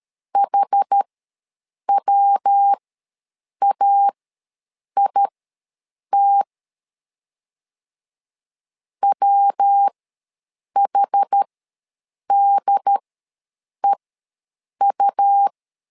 Morse code, heard as a keyed tone, about 13 wpm, spaced as for 7 wpm, 800 Hz. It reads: HWAIT WHDEU
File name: LEZIONI / 3 MWRO / 3 MW ws 12wpm 1816